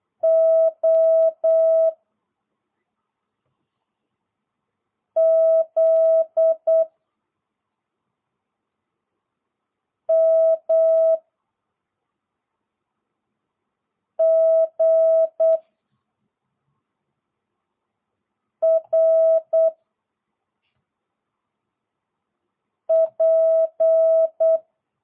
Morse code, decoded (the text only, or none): O Z M G R P